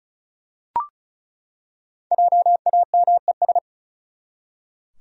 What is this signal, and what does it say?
Morse code, heard as a keyed tone, about 36 wpm, 700 Hz. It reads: JAMES
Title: USA Common Names - Max 5 Letters - 35wpm